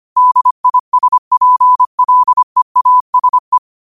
Morse code, heard as a keyed tone, about 25 wpm, 1000 Hz. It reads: DISPLEASE